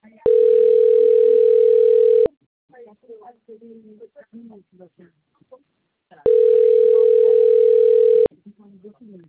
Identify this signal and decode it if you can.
Ringback tone (North America), 440 + 480 Hz, with speech.
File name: rg-1600-unknown- - - .24001